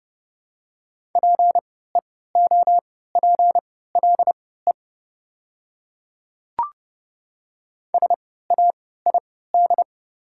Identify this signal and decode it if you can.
Morse code, heard as a keyed tone, about 30 wpm, spaced as for 17 wpm, 700 Hz. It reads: PEOPLE SAID
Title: 3x Character Spacing - Top 100 Words - 30wpm